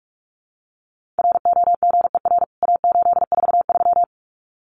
Morse code, heard as a keyed tone, about 44 wpm, 700 Hz. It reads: ROGER A843
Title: CWT Exchange 45wpm